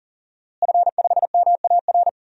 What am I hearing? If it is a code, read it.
Morse code, heard as a keyed tone, about 40 wpm, 690 Hz. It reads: F5MAR
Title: DXCC-100 Call Signs - 5 Characters - Repeated 2x - 40wpm